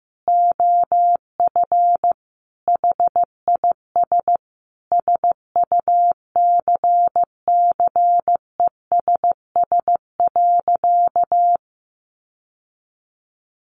Morse code, heard as a keyed tone, about 15 wpm, 700 Hz. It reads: OF HIS SUCCESS.
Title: Sentences from Top 900 Words 15wpm